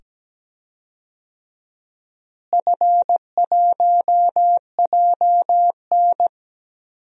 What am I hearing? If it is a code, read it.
Morse code, heard as a keyed tone, about 17 wpm, 700 Hz. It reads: F1JN
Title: DXCC-100 Call Signs - 4 Characters - Repeated 2x - 17wpm